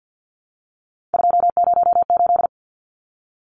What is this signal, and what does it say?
Morse code, heard as a keyed tone, about 50 wpm, 700 Hz. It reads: 208
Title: Numbers - 3-Digit 50wpm